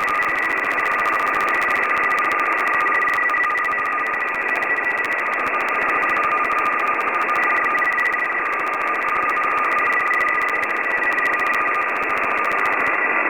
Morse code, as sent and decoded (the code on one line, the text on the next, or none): none